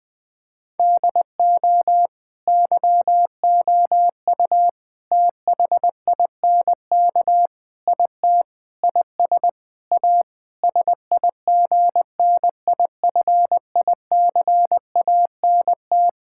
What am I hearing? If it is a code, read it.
Morse code, heard as a keyed tone, about 20 wpm, 700 Hz. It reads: DO YOU THINK IT IS A SIGNIFICANT
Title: Sentences from Top 1000 Words - Review Part 6 - 20wpm